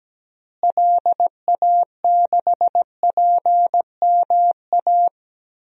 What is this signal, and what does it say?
Morse code, heard as a keyed tone, about 17 wpm, 700 Hz. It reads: LA6PMA
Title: DXCC-100 Call Signs - 6 Characters - Repeated 2x - 17wpm